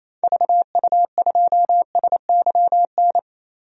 Morse code, heard as a keyed tone, about 28 wpm, 700 Hz. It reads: VU2SYN
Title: DXCC-100 Call Signs - 6 Characters - 28wpm